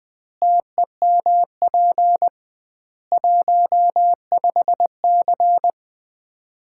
Morse code, heard as a keyed tone, about 20 wpm, 700 Hz. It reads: TEMP 15C
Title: Common QSO Phrases - 2x Word-Spacing - 20wpm